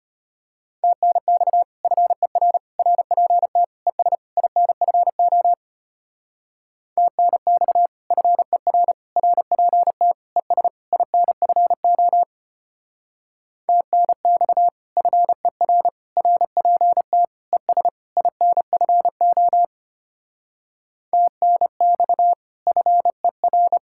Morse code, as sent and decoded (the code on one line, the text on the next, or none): - -. -..- / ..-. .-. / .-. .--. - / . ... / .. -. ..-. --- / - -. -..- / ..-. .-. / .-. .--. - / . ... / .. -. ..-. --- / - -. -..- / ..-. .-. / .-. .--. - / . ... / .. -. ..-. --- / - -. -..- / ..-. .-.
TNX FER RPT ES INFO TNX FER RPT ES INFO TNX FER RPT ES INFO TNX FER